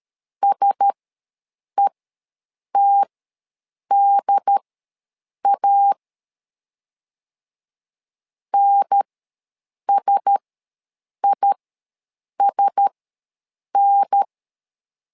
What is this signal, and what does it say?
Morse code, heard as a keyed tone, about 13 wpm, spaced as for 7 wpm, 800 Hz. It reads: SETDA NSISN
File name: LEZIONI / 4 Q1S ws 12wpm 1827